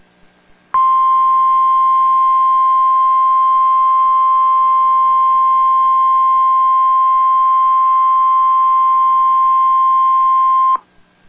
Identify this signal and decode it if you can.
Emergency Alert System (SAME) tones (the 1050 Hz attention tone). It is heard from 0.7 to 10.8 s.